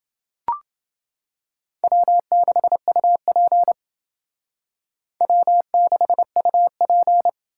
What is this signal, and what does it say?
Morse code, heard as a keyed tone, about 29 wpm, 700 Hz. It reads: W6UP W6UP